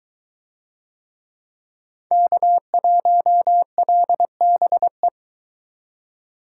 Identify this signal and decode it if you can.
Morse code, heard as a keyed tone, about 23 wpm, 700 Hz. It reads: K1LBE